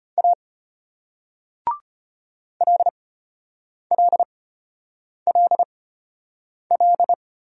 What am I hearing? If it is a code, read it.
Morse code, heard as a keyed tone, about 33 wpm, spaced as for 8 wpm, 700 Hz. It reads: A LLLL